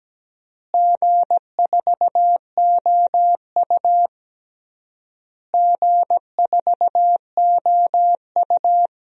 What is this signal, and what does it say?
Morse code, heard as a keyed tone, about 17 wpm, 700 Hz. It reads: G4OU G4OU
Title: DXCC-100 Call Signs - 4 Characters - Repeated 3x - 17wpm